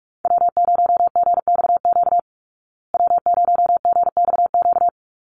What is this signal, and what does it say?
Morse code, heard as a keyed tone, about 44 wpm, 700 Hz. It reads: W0GXQ W0GXQ
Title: Call Signs Repeated 3x - 45wpm